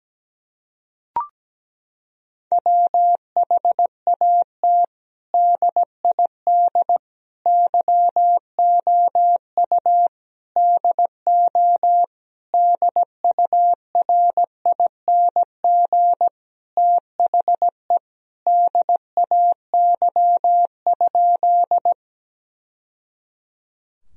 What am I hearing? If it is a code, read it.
Morse code, heard as a keyed tone, about 17 wpm, 700 Hz. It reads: WHAT DID YOU DO DURING THE DAY?
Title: Sentences from Top 1000 Words - Review Part 2 - 17wpm